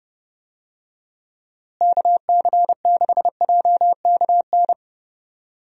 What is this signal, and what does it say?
Morse code, heard as a keyed tone, about 30 wpm, 700 Hz. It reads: KC6JKN
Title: USA Call Signs - 6 Characters - Repeated 2x - 30wpm